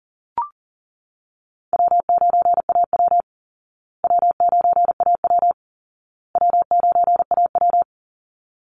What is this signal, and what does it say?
Morse code, heard as a keyed tone, about 40 wpm, 700 Hz. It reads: W9AW W9AW W9AW
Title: Easy Call Signs Repeated 3x - 40wpm